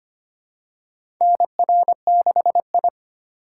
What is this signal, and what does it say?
Morse code, heard as a keyed tone, about 25 wpm, 700 Hz. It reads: NR6I
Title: USA Call Signs - 4 Characters - 2x Character Spacing - 25wpm